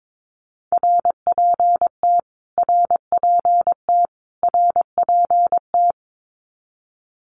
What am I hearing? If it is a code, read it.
Morse code, heard as a keyed tone, about 22 wpm, 700 Hz. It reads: RPT RPT RPT
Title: Abbreviations Repeated 3x - 22wpm